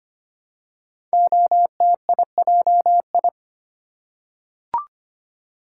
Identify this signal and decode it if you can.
Morse code, heard as a keyed tone, about 25 wpm, 700 Hz. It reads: OTIJI
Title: Code Groups - 5 Letters - 25wpm